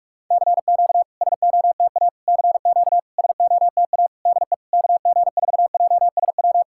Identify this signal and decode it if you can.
Morse code, heard as a keyed tone, about 41 wpm, 690 Hz. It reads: CQ SOTA CQ SOTA DE KG4JSW
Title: SOTA QSO Practice 45wpm